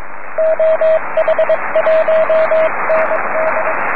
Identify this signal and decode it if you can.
Morse code, heard as a keyed tone, about 22 wpm, 640 Hz, with radio noise. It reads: OH1ND
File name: om3pv 70mhz / zaznam / oh1nd oh6pa